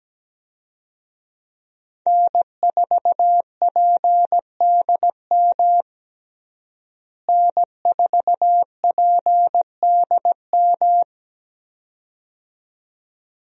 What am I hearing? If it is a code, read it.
Morse code, heard as a keyed tone, about 17 wpm, 700 Hz. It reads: N4PDM N4PDM
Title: USA Call Signs - 5 Characters - Repeated 2x - 17wpm